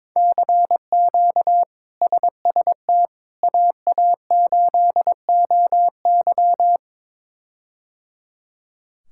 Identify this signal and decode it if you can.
Morse code, heard as a keyed tone, about 22 wpm, 700 Hz. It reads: CQ SST AA8OY
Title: K1USN SST QSO Practice - 22wpm